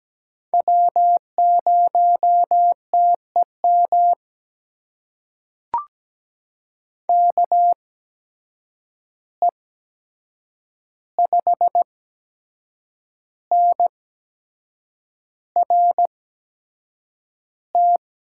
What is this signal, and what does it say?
Morse code, heard as a keyed tone, about 17 wpm, spaced as for 7 wpm, 700 Hz. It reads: W0TEM KE5NRT
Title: USA Call Signs - 8x Character Spacing - 17wpm